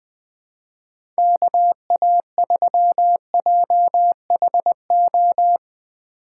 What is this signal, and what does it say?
Morse code, heard as a keyed tone, about 20 wpm, 700 Hz. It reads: KA3JHO
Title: USA Call Signs - 6 Characters - 6x Character Spacing - 20wpm